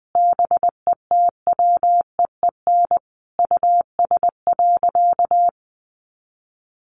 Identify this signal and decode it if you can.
Morse code, heard as a keyed tone, about 20 wpm, 700 Hz. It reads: BETWEEN US.
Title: Sentences from Top 500 Words 20wpm